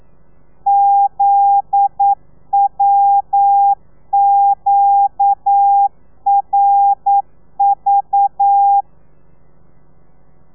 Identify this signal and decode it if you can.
Morse code, heard as a keyed tone, about 9 wpm, 800 Hz. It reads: ZWQRV
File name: telegrafia / i2tfj / Lez9 40